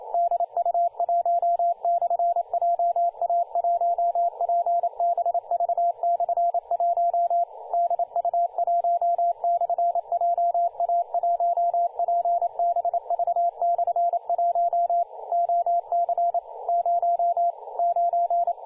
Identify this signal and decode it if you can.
Morse code, heard as a keyed tone, about 29 wpm, 690 Hz, with radio noise. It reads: DU1/JA1PBV/1 DU1/JA1PBV/1 OC 0 9
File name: du1-ja1pbv-1-28-040mhz-cw-cqing